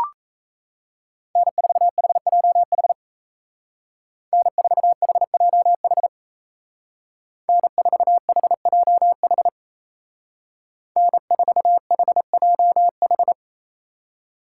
Morse code, heard as a keyed tone, about 33 wpm, 700 Hz. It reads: N4HJH N4HJH N4HJH N4HJH
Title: Speed Racing - USA 1x3 Call Signs 28wpm